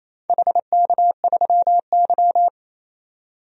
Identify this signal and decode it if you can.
Morse code, heard as a keyed tone, about 28 wpm, 690 Hz. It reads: HK3Y
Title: DXCC-100 Call Signs - 4 Characters - 28wpm